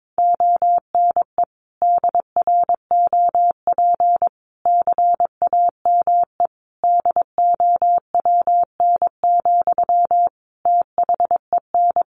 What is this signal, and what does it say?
Morse code, heard as a keyed tone, about 22 wpm, 700 Hz. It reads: ONE DROP CAME DOWN, THEN
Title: Sentences from Top 1400 Words 22wpm